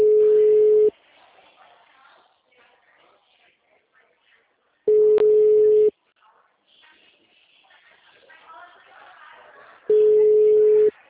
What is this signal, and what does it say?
Ringback tone (Europe, 425 Hz), with speech.